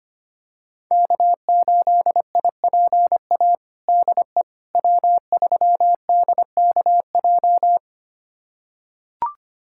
Morse code, heard as a keyed tone, about 25 wpm, 700 Hz. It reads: K8IPA DE W3DKJ
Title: USA Callsign DE Callsign - 5 Characters - 25wpm